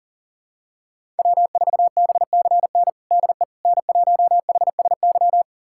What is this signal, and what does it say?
Morse code, heard as a keyed tone, about 40 wpm, 700 Hz. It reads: W4BCN DE N1HSY
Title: USA Callsign DE Callsign - 5 Characters - 40wpm